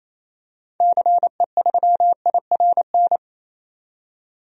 Morse code, heard as a keyed tone, about 28 wpm, 700 Hz. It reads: CE3IRN
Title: DXCC-100 Call Signs - 6 Characters - 28wpm